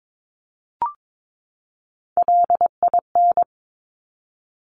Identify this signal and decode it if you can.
Morse code, heard as a keyed tone, about 22 wpm, 700 Hz. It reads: LIN
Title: Top Three Letter Combinations in Words 22wpm